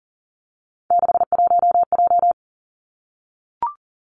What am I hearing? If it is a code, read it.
Morse code, heard as a keyed tone, about 40 wpm, 700 Hz. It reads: 61J
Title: Three Challenging Letter-Number Combinations 40wpm